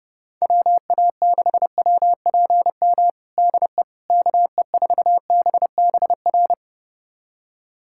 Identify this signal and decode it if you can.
Morse code, heard as a keyed tone, about 32 wpm, 700 Hz. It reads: WA6WPM DE KE4BBR
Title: USA Callsign DE Callsign - 6 Characters - 30wpm